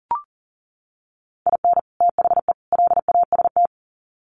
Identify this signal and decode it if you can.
Morse code, heard as a keyed tone, about 38 wpm, 700 Hz. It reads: IN THE LAST
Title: Sets of 3 Words 40wpm